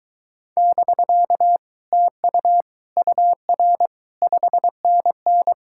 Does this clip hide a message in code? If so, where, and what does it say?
Yes: Morse code from 0.6 to 5.5 s, heard as a keyed tone, about 24 wpm, 700 Hz. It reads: <BK> TU UR 5NN